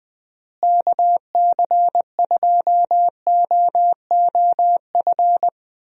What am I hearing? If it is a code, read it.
Morse code, heard as a keyed tone, about 21 wpm, 700 Hz. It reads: KC2OOF
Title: USA Call Signs - 6 Characters - Repeated 3x - 20wpm